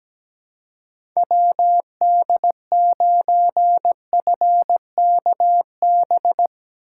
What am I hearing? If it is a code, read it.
Morse code, heard as a keyed tone, about 19 wpm, 700 Hz. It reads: WD9FKB
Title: Speed Racing - USA 2x3 Call Signs 17wpm